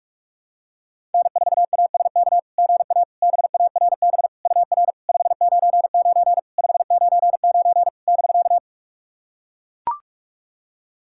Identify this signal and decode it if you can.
Morse code, heard as a keyed tone, about 44 wpm, 690 Hz. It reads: N4ASK GA BARB UR 599 599 <BK>
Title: SOTA QSO Practice 45wpm